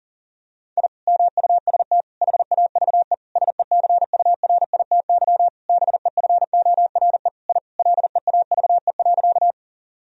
Morse code, heard as a keyed tone, about 37 wpm, 700 Hz. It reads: I MUST HAVE SECURITY BEFORE I LEAVE.
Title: Sentences from Top 1000 Words - Review Part 4 - 40wpm